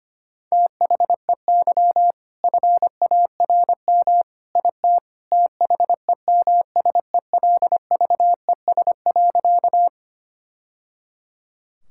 Morse code, heard as a keyed tone, about 25 wpm, 700 Hz. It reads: THEY FARM IT THEMSELVES.
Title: Sentences from Top 1000 Words - Review Part 5 - 25wpm